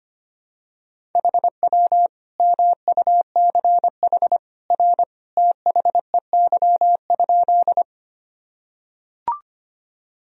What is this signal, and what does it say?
Morse code, heard as a keyed tone, about 25 wpm, 700 Hz. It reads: HW MUCH R THEY?
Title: Sentences from Top 200 Words and Abbreviations 25wpm